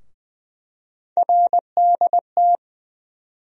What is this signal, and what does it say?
Morse code, heard as a keyed tone, about 20 wpm, 700 Hz. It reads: RDT
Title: USA Call Sign Postfixes - 3 Letters 20wpm